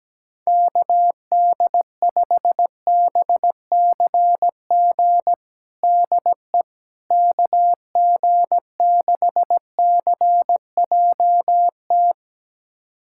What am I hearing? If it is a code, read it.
Morse code, heard as a keyed tone, about 17 wpm, 700 Hz. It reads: KD5BCG DE KG6CJT
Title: USA Callsign DE Callsign 17wpm